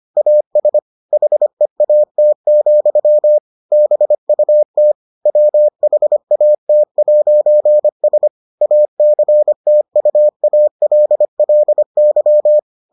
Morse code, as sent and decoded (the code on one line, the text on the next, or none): .- ... / .... .- - --..-- / -... ..- - / .-- .... .- - .----. ... / .- -.-. - ..- .- .-.. .-.. -.--
AS HEAT, BUT WHAT'S ACTUALLY